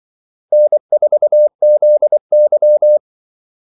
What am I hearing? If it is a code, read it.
Morse code, heard as a keyed tone, about 24 wpm, 600 Hz. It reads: N4ZY
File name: Callsigns-24wpm